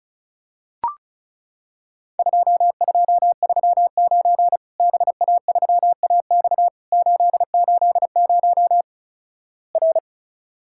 Morse code, heard as a keyed tone, about 36 wpm, 700 Hz. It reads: 2239 BA3AX 880 R
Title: Worked All Europe DX Contest - QTC Exchange - 35wpm